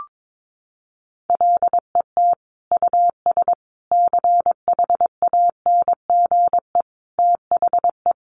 Morse code, heard as a keyed tone, about 22 wpm, 700 Hz. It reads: LET US CHANGE THE
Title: Sentences from Top 800 Words 22wpm